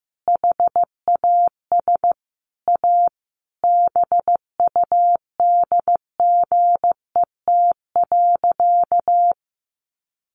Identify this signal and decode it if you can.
Morse code, heard as a keyed tone, about 15 wpm, 700 Hz. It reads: HAS A BUDGET.
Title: Sentences from Top 1200 Words 15wpm